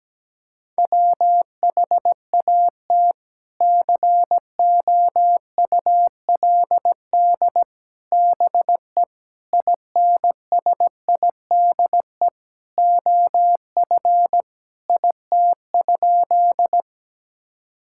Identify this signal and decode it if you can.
Morse code, heard as a keyed tone, about 17 wpm, 700 Hz. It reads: WHAT COULD BE INSIDE OF IT?
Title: Sentences from Top 1000 Words - Review Part 6 - 17wpm